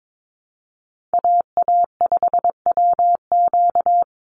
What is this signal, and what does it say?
Morse code, heard as a keyed tone, about 22 wpm, 700 Hz. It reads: AA5WQ
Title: USA Call Signs 22wpm